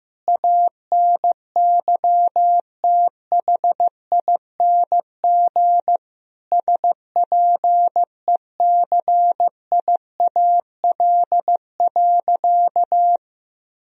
Morse code, heard as a keyed tone, about 15 wpm, 700 Hz. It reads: ANYTHING SPECIAL.
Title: Sentences from Top 1000 Words - Review Part 3 - 15wpm